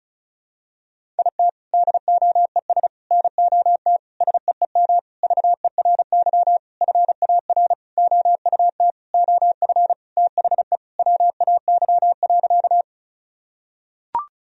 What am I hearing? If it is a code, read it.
Morse code, heard as a keyed tone, about 36 wpm, 700 Hz. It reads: IT DOES NOT SEEM VERY FAR OUT OF THE WAY.
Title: Sentences from Top 1000 Words - Review Part 1 - 35wpm